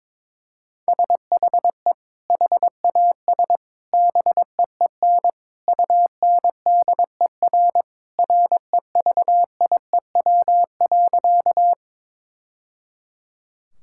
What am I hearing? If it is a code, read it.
Morse code, heard as a keyed tone, about 21 wpm, 700 Hz. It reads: SHE HAS BEEN UNDER REVIEW.
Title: Sentences from Top 1000 Words - Review Part 6 - 22wpm